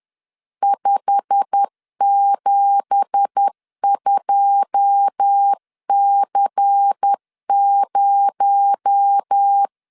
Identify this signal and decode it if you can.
Morse code, heard as a keyed tone, about 10 wpm, 800 Hz. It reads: 572C0
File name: LEZIONI / 10 SIGNS / 10B INT 10wpm 2844